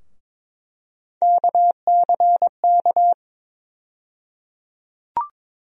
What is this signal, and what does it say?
Morse code, heard as a keyed tone, about 22 wpm, 700 Hz. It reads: KCK
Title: USA Call Sign Postfixes - 3 Letters 22wpm